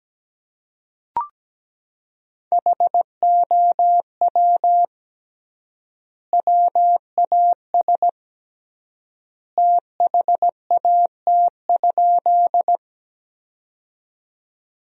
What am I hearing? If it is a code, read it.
Morse code, heard as a keyed tone, about 17 wpm, 700 Hz. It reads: HOW WAS THAT?
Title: 3x Word-Spacing - Sentences from Top 100 Words - 17wpm